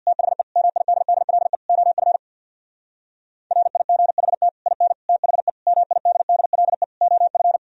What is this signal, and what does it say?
Morse code, heard as a keyed tone, about 63 wpm, 700 Hz. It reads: THE MIDDLE OF RIGHT IN THE MIDDLE OF